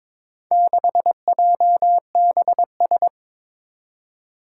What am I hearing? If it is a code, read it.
Morse code, heard as a keyed tone, about 22 wpm, 700 Hz. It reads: 6JBS
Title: Four Challenging Letter-Number Combinations 22wpm